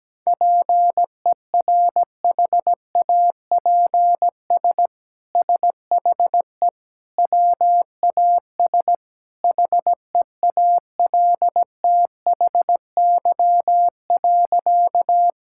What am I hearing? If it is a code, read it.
Morse code, heard as a keyed tone, about 17 wpm, 700 Hz. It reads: PERHAPS SHE WAS HEALTHY.
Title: Sentences from Top 1000 Words - Review Part 5 - 17wpm